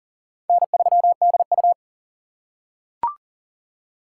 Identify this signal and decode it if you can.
Morse code, heard as a keyed tone, about 40 wpm, 700 Hz. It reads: N3DU
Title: USA Call Signs - 2x Character Spacing - 40wpm